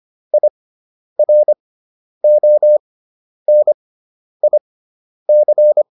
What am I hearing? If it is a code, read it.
Morse code, heard as a keyed tone, about 25 wpm, spaced as for 10 wpm, 600 Hz. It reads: IRONIC
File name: IR6 10